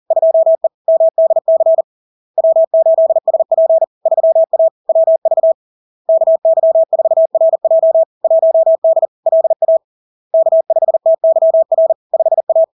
Morse code, heard as a keyed tone, about 40 wpm, 650 Hz. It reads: 2E MDC W8SP 3A WV KY4RJ 1D LA K5TYR 5A
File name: LARK-FD-2022-Log-in-CW-40wpm-650-Hz